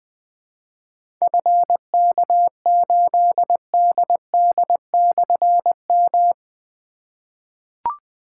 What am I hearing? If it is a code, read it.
Morse code, heard as a keyed tone, about 20 wpm, 700 Hz. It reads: FK8DD/M
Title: Difficult Call Signs 20wpm